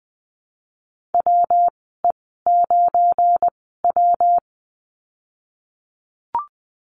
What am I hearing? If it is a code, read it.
Morse code, heard as a keyed tone, about 20 wpm, spaced as for 15 wpm, 700 Hz. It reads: WE9W
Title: 173 Lesson 28g - Number 9 - Call Signs 20wpm